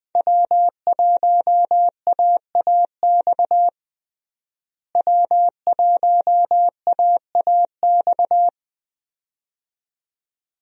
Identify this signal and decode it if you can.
Morse code, heard as a keyed tone, about 20 wpm, 700 Hz. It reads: W1AAX W1AAX
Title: USA Call Signs - 5 Characters - Repeated 2x - 20wpm